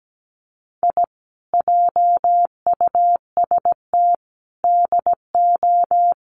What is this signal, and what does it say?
Morse code, heard as a keyed tone, about 17 wpm, 700 Hz. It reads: I JUST DO
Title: Sets of 5 Words Encore 17wpm